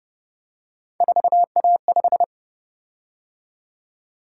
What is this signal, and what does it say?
Morse code, heard as a keyed tone, about 29 wpm, 700 Hz. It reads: 4A5